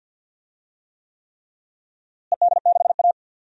Morse code, heard as a keyed tone, about 52 wpm, 700 Hz. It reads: ED6A
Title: DXCC-100 Call Signs - 4 Characters - Repeated 3x - 50wpm